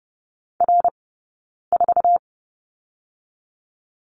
Morse code, heard as a keyed tone, about 30 wpm, 700 Hz. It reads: R 4
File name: course / 30:10wpm / 099 Lesson 17d - Letter u - Two Characters 30wpm